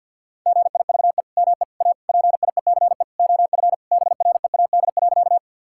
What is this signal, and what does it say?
Morse code, heard as a keyed tone, about 50 wpm, 700 Hz. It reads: GIVE ME A PIECE OF BREAD.
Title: Sentences from Top 1600 Words 50wpm